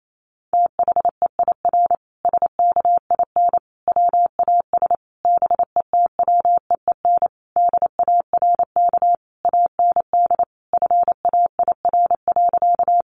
Morse code, heard as a keyed tone, about 28 wpm, 700 Hz. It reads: THEIR SKIN WAS BETWEEN DARK AND FAIR.
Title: Sentences from Top 1300 Words 28wpm